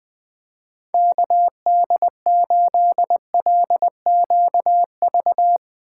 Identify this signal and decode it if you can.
Morse code, heard as a keyed tone, about 21 wpm, 700 Hz. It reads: KD8LQV
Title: Speed Racing - USA 2x3 Call Signs 17wpm